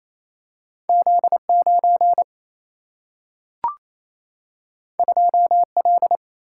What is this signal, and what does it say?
Morse code, heard as a keyed tone, about 28 wpm, 700 Hz. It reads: Z9 2L
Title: Single-Letter Single-Number 28wpm